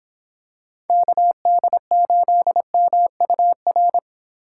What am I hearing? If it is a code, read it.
Morse code, heard as a keyed tone, about 26 wpm, 700 Hz. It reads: KD8MUR